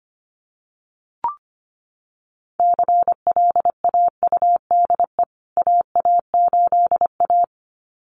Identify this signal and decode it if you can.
Morse code, heard as a keyed tone, about 25 wpm, 700 Hz. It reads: CLAUDE AA8A
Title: CWT Exchange 25wpm